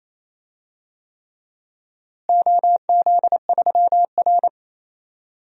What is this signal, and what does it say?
Morse code, heard as a keyed tone, about 28 wpm, 700 Hz. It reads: OZ3R
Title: DXCC-100 Call Signs - 4 Characters - Repeated 3x - 28wpm